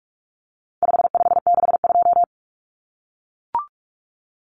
Morse code, heard as a keyed tone, about 44 wpm, 700 Hz. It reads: 5562
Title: Common Cut Numbers - 4-Digit 45wpm